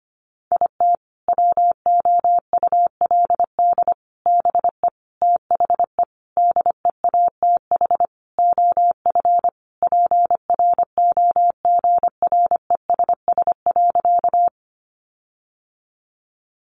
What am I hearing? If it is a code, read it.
Morse code, heard as a keyed tone, about 25 wpm, 700 Hz. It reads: IT WOULD BE THE DEATH OF PROGRESS.
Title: Sentences from Top 1400 Words 25wpm